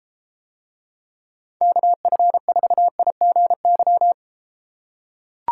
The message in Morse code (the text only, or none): KF4IGY